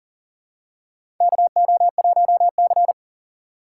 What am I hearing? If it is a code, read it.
Morse code, heard as a keyed tone, about 40 wpm, 700 Hz. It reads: KO1C
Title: USA Call Signs - 4 Characters - 4x Character Spacing - 40wpm